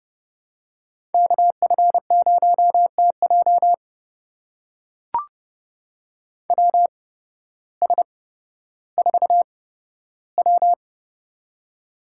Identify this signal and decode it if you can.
Morse code, heard as a keyed tone, about 29 wpm, spaced as for 14 wpm, 700 Hz. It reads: KF0TJ WS4W
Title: USA Call Signs - 8x Character Spacing - 30wpm